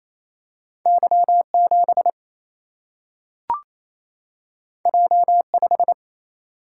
Morse code, heard as a keyed tone, about 28 wpm, 700 Hz. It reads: Y7 J5
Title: Single-Letter Single-Number 28wpm